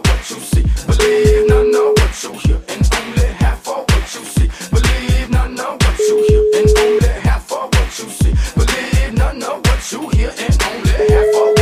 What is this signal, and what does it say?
Ringback tone (Europe, 425 Hz), with noise.